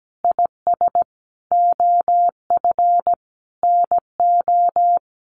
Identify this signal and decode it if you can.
Morse code, heard as a keyed tone, about 17 wpm, 700 Hz. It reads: IS OF NO
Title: Sentences from Top 1200 Words 17wpm